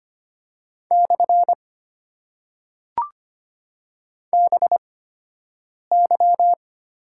Morse code, heard as a keyed tone, about 25 wpm, spaced as for 7 wpm, 690 Hz. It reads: / BY
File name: Top 90 QSO Elements - 8x Character Spacing - 25wpm